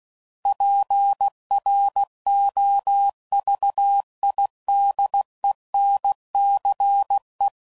Morse code, heard as a keyed tone, about 16 wpm, 800 Hz. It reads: PROVIDENCE
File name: CW Code Groups @15WPM / fromSM's / Cities